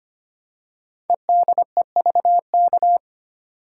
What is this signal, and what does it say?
Morse code, heard as a keyed tone, about 25 wpm, 700 Hz. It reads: EDEVK
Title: Code Groups - 5 Letters - 25wpm